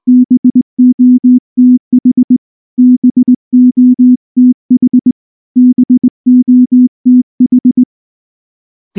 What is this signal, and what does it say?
Morse code, heard as a keyed tone, about 19 wpm, 250 Hz. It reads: BOTH BOTH BOTH
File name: morse / morse-training-20WPM-6